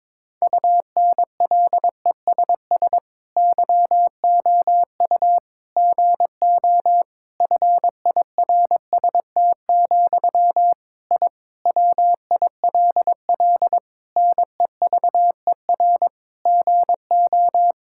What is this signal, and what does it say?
Morse code, heard as a keyed tone, about 22 wpm, 700 Hz. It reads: UNLESS YOU GO FIRST, I WILL NEVER GO
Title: Sentences from Top 1000 Words - Review Part 6 - 22wpm